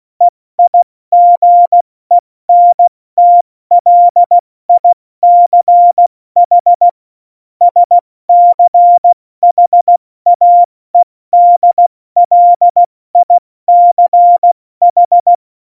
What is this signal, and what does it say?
Morse code, heard as a keyed tone, about 16 wpm, 700 Hz. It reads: EIGENTLICH SCHAEDLICH
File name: Morsekurs / Witze / Witz 08. Januar 2019, 70 F80